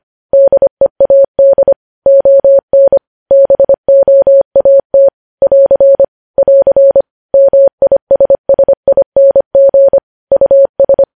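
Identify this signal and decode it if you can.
Morse code, heard as a keyed tone, about 25 wpm, 550 Hz. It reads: DEAD ON BOAT <AR> <AR> MISSING US
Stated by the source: Morse code news headlines at 25, opening with vvv.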